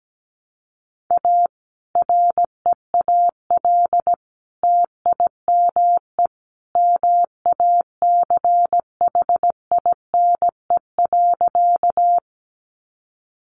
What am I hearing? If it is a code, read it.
Morse code, heard as a keyed tone, about 17 wpm, 700 Hz. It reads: A REAL TIME MACHINE.
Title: A Sound Of Thunder 17wpm